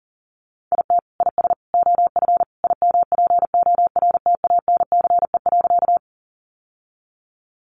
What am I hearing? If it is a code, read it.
Morse code, heard as a keyed tone, about 40 wpm, 700 Hz. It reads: IT IS OF IMPORTANCE.
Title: Sentences from Top 1200 Words 40wpm